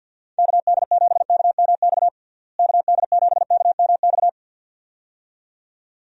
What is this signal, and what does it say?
Morse code, heard as a keyed tone, about 50 wpm, 700 Hz. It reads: KD7KMX KD7KMX
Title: USA Call Signs - 6 Characters - Repeated 2x - 50wpm